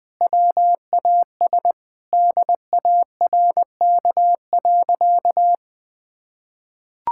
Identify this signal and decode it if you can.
Morse code, heard as a keyed tone, about 20 wpm, 700 Hz. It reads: WAS DARK.
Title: Sentences with 4-Word Phrases - 20wpm